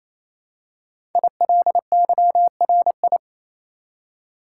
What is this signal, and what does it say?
Morse code, heard as a keyed tone, about 28 wpm, 700 Hz. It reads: ILYRI
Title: Code Groups - 5 Letters - 28wpm